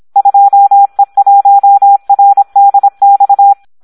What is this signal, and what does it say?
Morse code, heard as a keyed tone, about 26 wpm, 800 Hz. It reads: 2E1RDX
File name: Audio practica CW